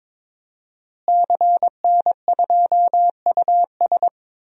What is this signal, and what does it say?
Morse code, heard as a keyed tone, about 22 wpm, 700 Hz. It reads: CN2US
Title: DXCC-100 Call Signs - 5 Characters - Repeated 2x - 22wpm